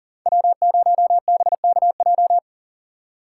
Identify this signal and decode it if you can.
Morse code, heard as a keyed tone, about 49 wpm, 700 Hz. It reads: W0BKJ
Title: Speed Racing - USA 1x3 Call Signs 40wpm